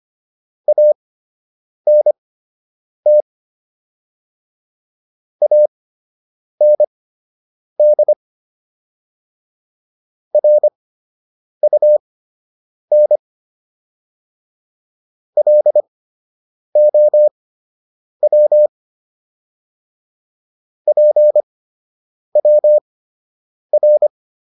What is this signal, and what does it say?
Morse code, heard as a keyed tone, about 25 wpm, spaced as for 8 wpm, 600 Hz. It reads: ANT AND RUN LOW PWR